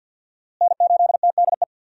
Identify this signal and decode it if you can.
Morse code, heard as a keyed tone, about 53 wpm, 700 Hz. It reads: N8TDE
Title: USA Call Signs - 5 Characters - Repeated 3x - 50wpm